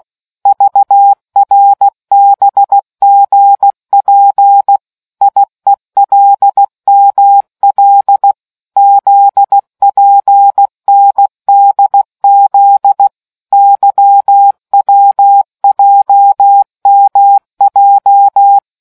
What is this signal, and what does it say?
Morse code, heard as a keyed tone, about 16 wpm, 800 Hz. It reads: VRBGP IELML ZPNDZ YWJMJ